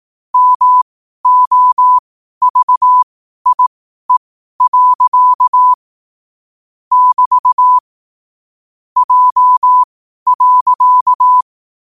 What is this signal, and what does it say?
Morse code, heard as a keyed tone, about 18 wpm, spaced as for 12 wpm, 1000 Hz. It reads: MOVIE. <BT> J.
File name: qotdfiles / 12wpm